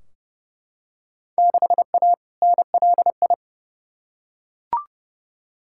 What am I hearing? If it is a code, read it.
Morse code, heard as a keyed tone, about 30 wpm, 700 Hz. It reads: 6A NLI
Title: Field Day Exchanges Encore - 30wpm